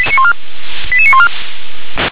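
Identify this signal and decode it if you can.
Five-tone selective call (ZVEI), heard as quick runs of five tones, tones of about 80 ms. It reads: 88914
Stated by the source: ZVEI five-tone call received by SDR.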